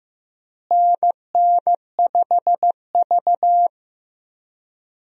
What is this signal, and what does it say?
Morse code, heard as a keyed tone, about 15 wpm, 700 Hz. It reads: NN5V
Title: USA Call Signs - 4 Characters - Repeated 3x - 15wpm